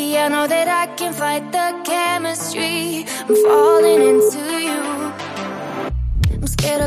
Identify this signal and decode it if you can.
Ringback tone (Europe, 425 Hz), with noise.